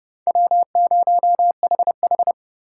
Morse code, heard as a keyed tone, about 30 wpm, 700 Hz. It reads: W0HH
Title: USA Call Signs - 4 Characters - 4x Character Spacing - 30wpm